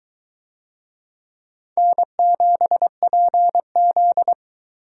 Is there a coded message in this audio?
Yes: Morse code from 1.8 to 4.3 s, heard as a keyed tone, about 21 wpm, 700 Hz. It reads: N7PZ